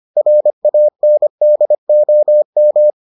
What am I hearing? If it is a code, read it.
Morse code, heard as a keyed tone, about 25 wpm, 600 Hz. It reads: RANDOM